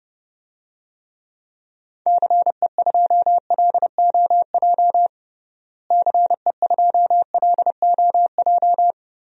Morse code, heard as a keyed tone, about 32 wpm, 700 Hz. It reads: CE2LOJ CE2LOJ
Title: DXCC-100 Call Signs - 6 Characters - Repeated 3x - 30wpm